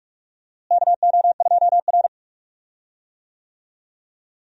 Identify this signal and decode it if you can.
Morse code, heard as a keyed tone, about 41 wpm, 700 Hz. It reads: KO2R
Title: USA Call Signs - 4 Characters - Repeated 3x - 45wpm